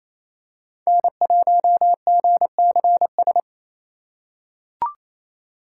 Morse code, heard as a keyed tone, about 29 wpm, 700 Hz. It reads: N1GCS
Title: USA Call Signs - 5 Characters - 4x Character Spacing - 28wpm